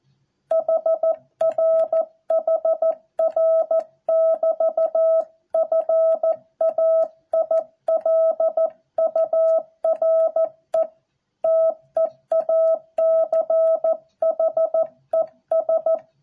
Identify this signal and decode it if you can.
Morse code, heard as a keyed tone, about 14 wpm, 650 Hz. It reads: HR HR <BT> FAILURE TEACHES